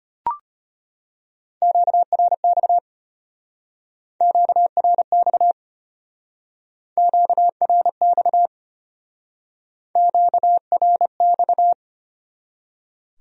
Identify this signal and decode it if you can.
Morse code, heard as a keyed tone, about 32 wpm, 700 Hz. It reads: QRX QRX QRX QRX